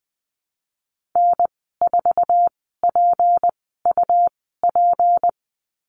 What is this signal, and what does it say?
Morse code, heard as a keyed tone, about 20 wpm, spaced as for 15 wpm, 700 Hz. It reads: N4PUP
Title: 200 Lesson 32g - Letter k - Call Signs 20wpm